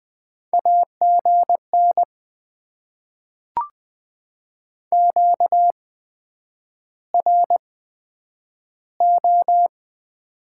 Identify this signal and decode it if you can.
Morse code, heard as a keyed tone, about 20 wpm, spaced as for 9 wpm, 700 Hz. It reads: AGN QRO
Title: Top 90 QSO Elements - 8x Character Spacing - 20wpm